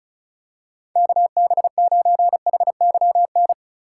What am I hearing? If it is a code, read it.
Morse code, heard as a keyed tone, about 36 wpm, 700 Hz. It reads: KB9HYN
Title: USA Call Signs - 6 Characters - 3x Character Spacing - 35wpm